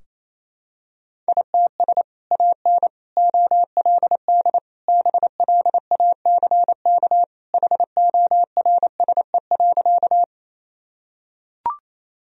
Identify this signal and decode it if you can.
Morse code, heard as a keyed tone, about 28 wpm, 700 Hz. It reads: ITS AN OLD BLACK HORSE.
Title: Sentences from Top 1000 Words - Review Part 2 - 28wpm